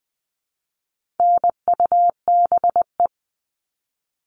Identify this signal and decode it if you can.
Morse code, heard as a keyed tone, about 20 wpm, 700 Hz. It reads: NUBE
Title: All Cut Numbers - 4-Digit 20wpm